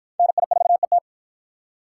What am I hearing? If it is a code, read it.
Morse code, heard as a keyed tone, about 52 wpm, 700 Hz. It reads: NI4ET